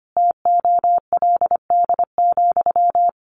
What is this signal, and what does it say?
Morse code, heard as a keyed tone, about 25 wpm, 700 Hz. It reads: TOLD,
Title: Sentences from Top 1400 Words 25wpm